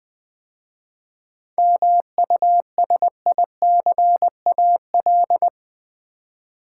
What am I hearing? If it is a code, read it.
Morse code, heard as a keyed tone, about 20 wpm, 700 Hz. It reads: MUSICAL
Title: 7 Letter Words - First 3 Letters Given - 20wpm